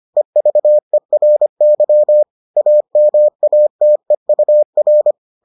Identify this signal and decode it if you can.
Morse code, heard as a keyed tone, about 25 wpm, 600 Hz. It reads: EVERY AMATEUR